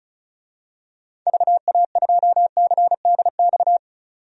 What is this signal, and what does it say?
Morse code, heard as a keyed tone, about 34 wpm, 700 Hz. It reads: VA2CDX
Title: DXCC-100 Call Signs - 6 Characters - 35wpm